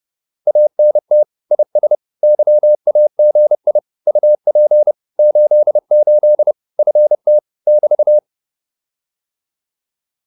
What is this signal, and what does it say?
Morse code, heard as a keyed tone, about 30 wpm, 600 Hz. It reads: ANT IS YAGI UP 88 FT <BT>